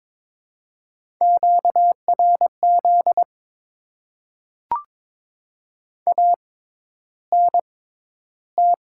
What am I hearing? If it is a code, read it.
Morse code, heard as a keyed tone, about 22 wpm, spaced as for 11 wpm, 700 Hz. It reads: QRZ ANT
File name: Top 90 QSO Elements - 6x Character Spacing - 22wpm